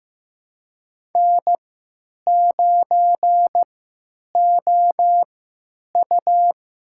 Morse code, heard as a keyed tone, about 16 wpm, spaced as for 9 wpm, 700 Hz. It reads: N9OU
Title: USA Call Signs - 3x Character Spacing - 15wpm